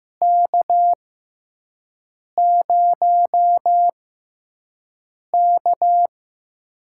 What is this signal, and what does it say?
Morse code, heard as a keyed tone, about 15 wpm, spaced as for 5 wpm, 700 Hz. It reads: K0K
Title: USA Call Signs - 4 Characters - 6x Character Spacing - 15wpm